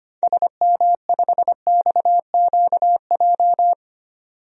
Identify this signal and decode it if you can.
Morse code, heard as a keyed tone, about 25 wpm, 700 Hz. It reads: SM5XQJ
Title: DXCC-100 Call Signs - 6 Characters - 25wpm